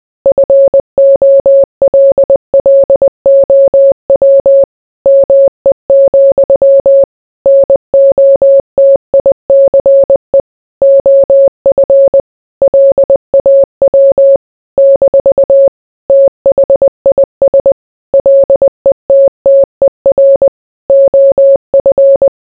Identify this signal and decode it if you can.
Morse code, heard as a keyed tone, about 20 wpm, 550 Hz. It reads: FOLLOW ME, NOTICE OF LAW - THIS LETTER OF